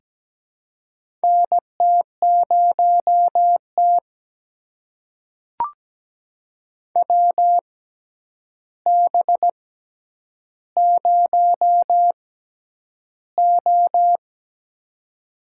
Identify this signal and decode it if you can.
Morse code, heard as a keyed tone, about 17 wpm, spaced as for 9 wpm, 700 Hz. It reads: NT0T WB0O
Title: USA Call Signs - 4 Characters - 6x Character Spacing - 17wpm